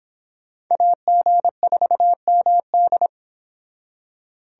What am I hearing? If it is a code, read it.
Morse code, heard as a keyed tone, about 25 wpm, 700 Hz. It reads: AG4MD